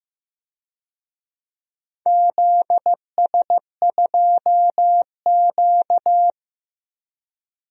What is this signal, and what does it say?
Morse code, heard as a keyed tone, about 15 wpm, 700 Hz. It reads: ZS2Q
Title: DXCC-100 Call Signs - 4 Characters - Repeated 3x - 15wpm